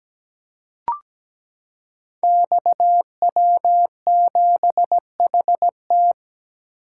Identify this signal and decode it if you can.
Morse code, heard as a keyed tone, about 17 wpm, 700 Hz. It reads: XW7HT
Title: Code Groups - 5 Letter:Numbers - 17wpm